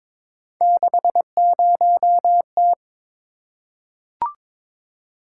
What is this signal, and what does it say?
Morse code, heard as a keyed tone, about 22 wpm, 700 Hz. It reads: 60T
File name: practice / 22wpm / Three Letter-Number Combinations - 22wpm